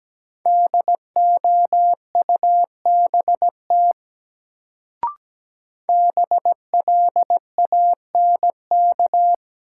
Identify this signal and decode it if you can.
Morse code, heard as a keyed tone, about 17 wpm, 700 Hz. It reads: DOUBT BLANK
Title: Sentences from Top 1500 Words - New Word Drill - 17wpm